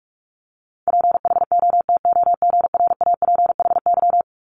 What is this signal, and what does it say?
Morse code, heard as a keyed tone, about 46 wpm, 700 Hz. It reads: PHOTOGRAPHY
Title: Multi-letter Phonemes 45wpm